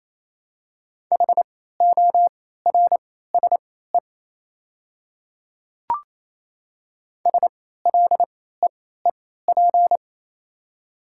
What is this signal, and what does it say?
Morse code, heard as a keyed tone, about 28 wpm, spaced as for 16 wpm, 700 Hz. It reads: HORSE SLEEP
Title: 3x Character Spacing - Top 500 Words - 28wpm